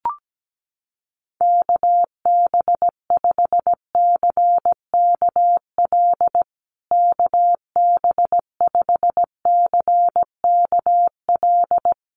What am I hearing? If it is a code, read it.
Morse code, heard as a keyed tone, about 17 wpm, 700 Hz. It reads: KB5CKL KB5CKL
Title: USA Call Signs Repeated 3x 17wpm